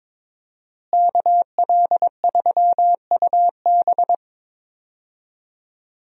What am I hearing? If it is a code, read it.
Morse code, heard as a keyed tone, about 22 wpm, 700 Hz. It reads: KL3UB
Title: USA Call Signs - 5 Characters - Repeated 2x - 22wpm